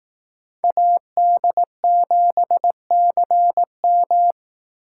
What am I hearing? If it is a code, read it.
Morse code, heard as a keyed tone, about 18 wpm, 700 Hz. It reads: AD7CM
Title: Speed Racing - USA 2x2 Call Signs 15wpm